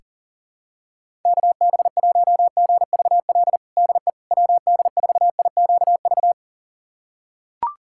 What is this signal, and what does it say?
Morse code, heard as a keyed tone, about 40 wpm, 700 Hz. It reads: KB1GVL DE WD4IQV
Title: USA Callsign DE Callsign - 6 Characters - 40wpm